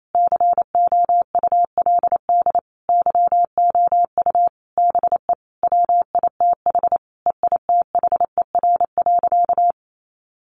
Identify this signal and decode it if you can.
Morse code, heard as a keyed tone, about 28 wpm, 700 Hz. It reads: COULD YOU BE WITH EITHER.
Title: Sentences from Top 1000 Words 28wpm